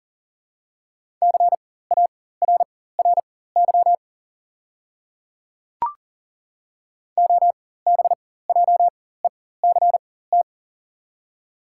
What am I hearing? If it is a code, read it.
Morse code, heard as a keyed tone, about 41 wpm, spaced as for 19 wpm, 690 Hz. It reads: CARRY OBJECT
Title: 4x Character Spacing - Top 500 Words - 40wpm